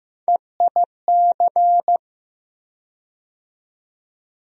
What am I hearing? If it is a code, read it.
Morse code, heard as a keyed tone, about 15 wpm, 700 Hz. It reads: EIC
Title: USA Call Sign Postfixes - 3 Letters 15wpm